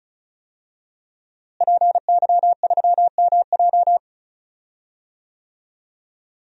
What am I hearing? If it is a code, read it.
Morse code, heard as a keyed tone, about 36 wpm, 700 Hz. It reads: PY3MJ
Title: DXCC-100 Call Signs - 5 Characters - Repeated 2x - 35wpm